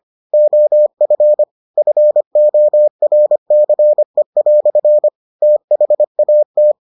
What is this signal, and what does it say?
Morse code, heard as a keyed tone, about 25 wpm, 600 Hz. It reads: OF FORCE" THAT